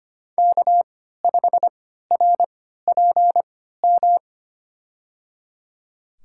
Morse code, heard as a keyed tone, about 25 wpm, spaced as for 14 wpm, 700 Hz. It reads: K5RPM
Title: USA Call Signs - 5 Characters - 3x Character Spacing - 25wpm